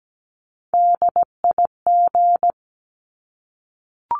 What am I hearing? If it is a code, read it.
Morse code, heard as a keyed tone, about 17 wpm, 700 Hz. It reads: DIG
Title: Top 1501-1750 Words 17wpm